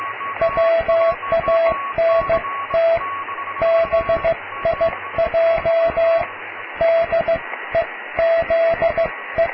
Morse code, heard as a keyed tone, about 15 wpm, 660 Hz, with radio noise. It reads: WANT BIJ DEZE
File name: cw / tekst(a)+QRM